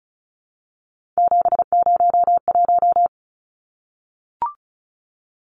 Morse code, heard as a keyed tone, about 34 wpm, 700 Hz. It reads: 701
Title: Numbers - 3-Digit 35wpm